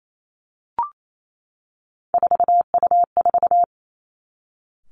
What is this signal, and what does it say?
Morse code, heard as a keyed tone, about 28 wpm, 700 Hz. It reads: 4U4
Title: Three Challenging Letter-Number Combinations 28wpm